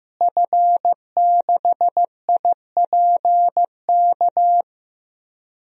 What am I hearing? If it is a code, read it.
Morse code, heard as a keyed tone, about 15 wpm, 700 Hz. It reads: F6IPK
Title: DXCC-100 Call Signs - 5 Characters - Repeated 2x - 15wpm